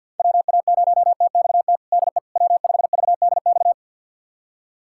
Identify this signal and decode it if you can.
Morse code, heard as a keyed tone, about 50 wpm, 700 Hz. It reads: WA0TXT DE W5VDX
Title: USA Callsign DE Callsign 50wpm